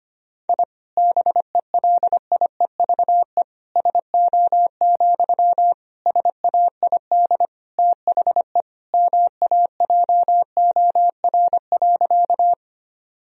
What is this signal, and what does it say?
Morse code, heard as a keyed tone, about 25 wpm, 700 Hz. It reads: I BELIEVE SO, SAID THE MAJOR.
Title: Sentences from Top 1000 Words - Review Part 4 - 25wpm